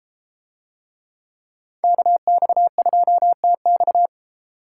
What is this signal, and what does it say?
Morse code, heard as a keyed tone, about 32 wpm, 700 Hz. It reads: KX2TX